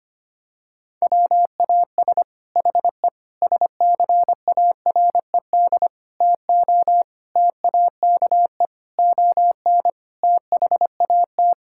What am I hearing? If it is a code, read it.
Morse code, heard as a keyed tone, about 25 wpm, 700 Hz. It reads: WAS HE SCARED TO TAKE ON THAT
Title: Sentences from Top 1000 Words - Review Part 6 - 25wpm